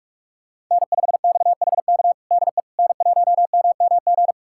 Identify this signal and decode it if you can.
Morse code, heard as a keyed tone, about 44 wpm, 700 Hz. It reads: N5XHK DE N1MMG
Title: USA Callsign DE Callsign - 5 Characters - 45wpm